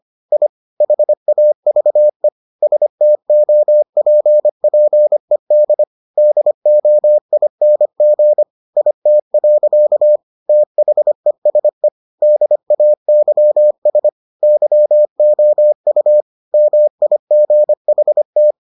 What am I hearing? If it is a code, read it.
Morse code, heard as a keyed tone, about 25 wpm, 600 Hz. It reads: I HAVE STOPPED DOING IT. THESE DAYS YOU MIGHT